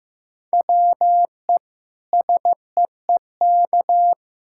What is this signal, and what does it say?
Morse code, heard as a keyed tone, about 15 wpm, 700 Hz. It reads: WE SEEK
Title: Sentences from Top 1600 Words 15wpm